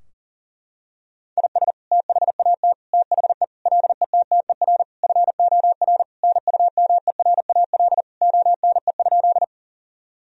Morse code, heard as a keyed tone, about 40 wpm, 700 Hz. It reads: IS THAT THE LETTER FOR NUMERAL ONE?
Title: Sentences from Top 1000 Words - Review Part 2 - 40wpm